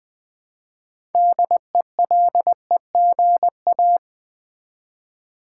Morse code, heard as a keyed tone, about 20 wpm, 700 Hz. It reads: DELEGA
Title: Build A Word - 3 Letters At A Time - 8-Letter Words - 20wpm